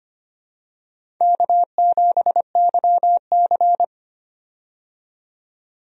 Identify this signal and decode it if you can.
Morse code, heard as a keyed tone, about 25 wpm, 700 Hz. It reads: K7YC
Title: USA Call Signs - 4 Characters - 25wpm